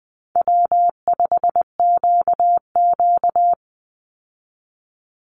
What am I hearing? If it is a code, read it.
Morse code, heard as a keyed tone, about 20 wpm, 700 Hz. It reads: W5QQ
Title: Easy Call Signs Repeated 3x - 20wpm